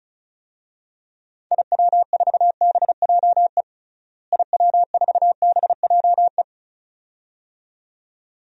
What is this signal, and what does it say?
Morse code, heard as a keyed tone, about 36 wpm, 700 Hz. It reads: IW4BJE IW4BJE
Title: DXCC-100 Call Signs - 6 Characters - Repeated 2x - 35wpm